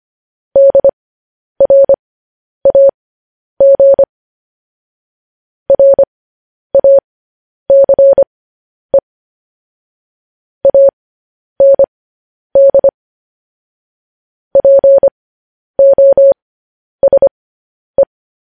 Morse code, heard as a keyed tone, about 25 wpm, spaced as for 10 wpm, 550 Hz. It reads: DRAG RACE AND POSE
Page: news headlines in morse code at 10 wpm